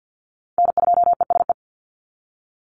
Morse code, heard as a keyed tone, about 49 wpm, 700 Hz. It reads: N2ESE
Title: NA SOTA Chaser Callsigns 50wpm